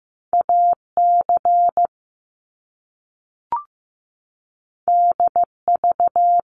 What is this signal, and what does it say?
Morse code, heard as a keyed tone, about 15 wpm, 700 Hz. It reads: AC DV
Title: Top Two Letter Combinations in Words 15wpm